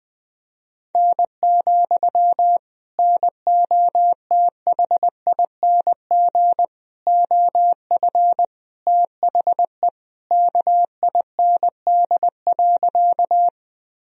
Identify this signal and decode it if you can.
Morse code, heard as a keyed tone, about 20 wpm, 700 Hz. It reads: N, NOTHING OF THE KIND.
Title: Sentences from Top 500 Words and Abbreviations 20wpm